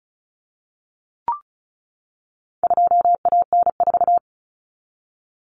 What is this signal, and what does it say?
Morse code, heard as a keyed tone, about 34 wpm, 700 Hz. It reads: 2AN4
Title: Common Cut Numbers - 4-Digit 35wpm